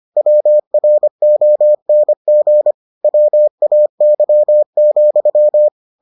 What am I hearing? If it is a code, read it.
Morse code, heard as a keyed tone, about 25 wpm, 600 Hz. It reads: WRONG WAY,